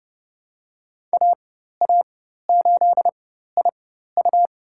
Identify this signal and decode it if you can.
Morse code, heard as a keyed tone, about 32 wpm, spaced as for 14 wpm, 700 Hz. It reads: AA8IU
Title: USA Call Signs - 5 Characters - 4x Character Spacing - 30wpm